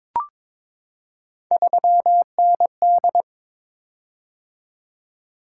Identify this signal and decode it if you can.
Morse code, heard as a keyed tone, about 22 wpm, 700 Hz. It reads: 3ND
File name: practice / 22wpm / Three Letter-Number Combinations - 22wpm